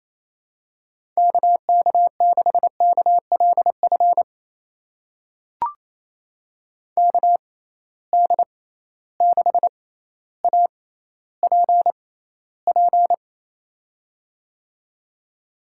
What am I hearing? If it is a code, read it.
Morse code, heard as a keyed tone, about 28 wpm, spaced as for 14 wpm, 700 Hz. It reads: KK6KLF KD6APP
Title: USA Call Signs - 6 Characters - 6x Character Spacing - 28wpm